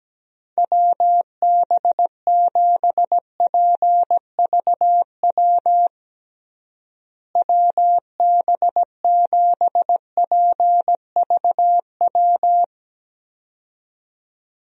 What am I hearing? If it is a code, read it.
Morse code, heard as a keyed tone, about 17 wpm, 700 Hz. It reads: WB7PVW WB7PVW
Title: USA Call Signs - 6 Characters - Repeated 2x - 17wpm